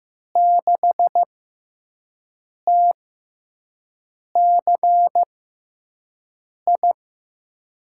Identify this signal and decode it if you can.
Morse code, heard as a keyed tone, about 15 wpm, spaced as for 5 wpm, 700 Hz. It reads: 6TCI
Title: USA Call Signs - 6 Characters - 6x Character Spacing - 15wpm